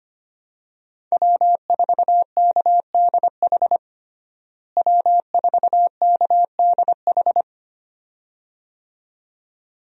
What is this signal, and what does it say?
Morse code, heard as a keyed tone, about 25 wpm, 700 Hz. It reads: W4KDH W4KDH
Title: USA Call Signs - 5 Characters - Repeated 2x - 25wpm